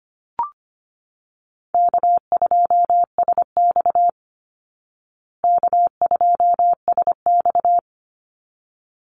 Morse code, heard as a keyed tone, about 25 wpm, 700 Hz. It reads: K2SX K2SX
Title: Call Signs Repeated 3x - 25wpm